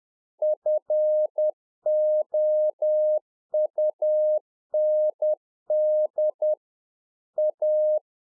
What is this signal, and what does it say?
Morse code, heard as a keyed tone, about 10 wpm, 610 Hz. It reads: FOUND A
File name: UNFOGETTABLE QSO 10WPM